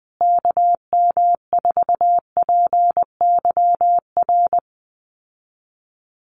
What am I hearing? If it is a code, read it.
Morse code, heard as a keyed tone, about 20 wpm, 700 Hz. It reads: KM4PYR
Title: USA Call Signs Repeated 2x 20wpm